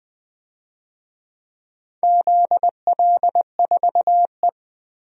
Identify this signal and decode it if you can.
Morse code, heard as a keyed tone, about 20 wpm, 700 Hz. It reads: ZL4E
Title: DXCC-100 Call Signs - 4 Characters - Repeated 2x - 20wpm